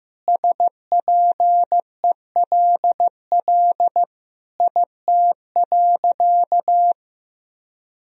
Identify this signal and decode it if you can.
Morse code, heard as a keyed tone, about 15 wpm, 700 Hz. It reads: SPELL IT.
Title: Sentences from Top 1000 Words - Review Part 1 - 15wpm